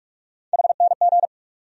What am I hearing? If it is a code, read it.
Morse code, heard as a keyed tone, about 46 wpm, 700 Hz. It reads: HNG
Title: USA Call Sign Postfixes - 3 Letters 45wpm